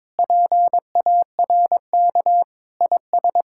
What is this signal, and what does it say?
Morse code, heard as a keyed tone, about 22 wpm, 700 Hz. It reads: PARK IS
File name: practice / 22wpm / Sentences with 2-Word Phrases - 22wpm